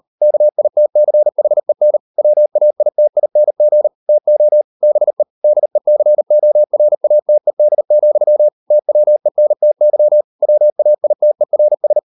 Morse code, heard as a keyed tone, about 38 wpm, 600 Hz. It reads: KITCHEN WAITING TO BE DECORATED, TWENTY WAITERS